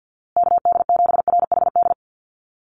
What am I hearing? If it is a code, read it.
Morse code, heard as a keyed tone, about 48 wpm, 700 Hz. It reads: KD7RHD
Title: USA Call Signs Repeated 3x 50wpm